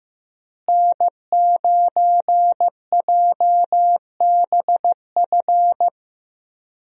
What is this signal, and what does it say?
Morse code, heard as a keyed tone, about 15 wpm, 700 Hz. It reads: N9JBF
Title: USA Call Signs - 5 Characters - Repeated 2x - 15wpm